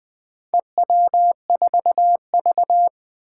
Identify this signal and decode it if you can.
Morse code, heard as a keyed tone, about 20 wpm, 700 Hz. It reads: EW4V
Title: Four Challenging Letter-Number Combinations 20wpm